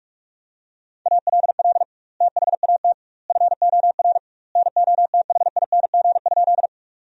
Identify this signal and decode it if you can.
Morse code, heard as a keyed tone, about 46 wpm, 700 Hz. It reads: ALL THAT FOR NOTHING?
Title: 2x Word-Spacing - Sentences from Top 500 Words - 45wpm